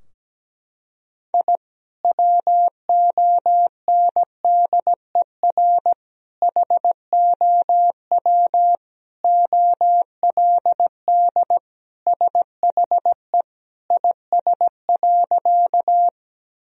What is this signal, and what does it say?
Morse code, heard as a keyed tone, about 17 wpm, 700 Hz. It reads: I WONDER HOW OLD SHE IS.
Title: Sentences from Top 1000 Words - Review Part 3 - 17wpm